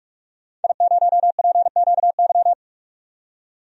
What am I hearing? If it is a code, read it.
Morse code, heard as a keyed tone, about 46 wpm, 700 Hz. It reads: I0PQY
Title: DXCC-100 Call Signs - 5 Characters - 45wpm